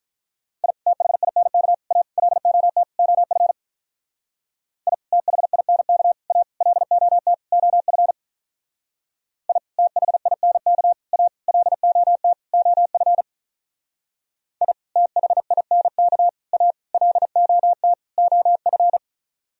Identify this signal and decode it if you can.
Morse code, heard as a keyed tone, about 43 wpm, 700 Hz. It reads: I THINK A LOT OF I THINK A LOT OF I THINK A LOT OF I THINK A LOT OF
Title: Speed-Racing - Sets of 5 Words 35wpm